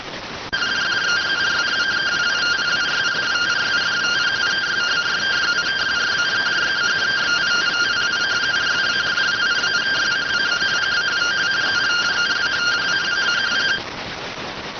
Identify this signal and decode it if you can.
Two-tone teleprinter signal, about 170 Hz shift.